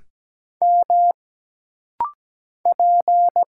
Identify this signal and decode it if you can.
Morse code, heard as a keyed tone, about 17 wpm, 700 Hz. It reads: M P
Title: Single Letters - Rapid-Fire 17wpm